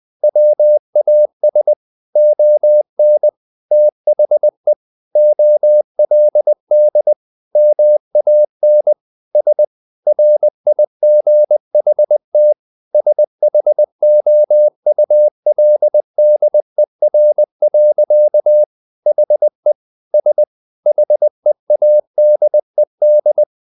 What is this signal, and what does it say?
Morse code, heard as a keyed tone, about 20 wpm, 600 Hz. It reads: WAS ON THE OLD MAN S RIGHT SHOULDER. HE S HEADED
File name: the old man and the sea / the old man and the sea part 1 20wpm